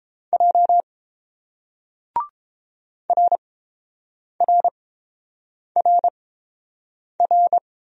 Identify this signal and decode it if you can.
Morse code, heard as a keyed tone, about 29 wpm, spaced as for 7 wpm, 690 Hz. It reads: J RRRR